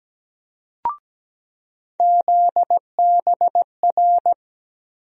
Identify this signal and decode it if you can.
Morse code, heard as a keyed tone, about 17 wpm, 700 Hz. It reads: ZBR
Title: USA Call Sign Postfixes - 3 Letters 17wpm